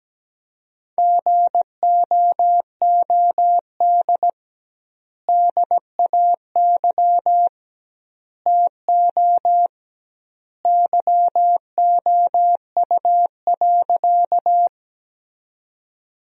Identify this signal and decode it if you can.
Morse code, heard as a keyed tone, about 17 wpm, 700 Hz. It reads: GOOD DAY TO YOU.
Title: 2x Word-Spacing - Sentences from Top 200 Words - 17wpm